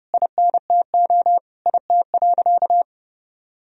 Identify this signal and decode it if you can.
Morse code, heard as a keyed tone, about 30 wpm, 700 Hz. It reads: INTO IT.
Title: Sentences from Top 1000 Words - Review Part 1 - 30wpm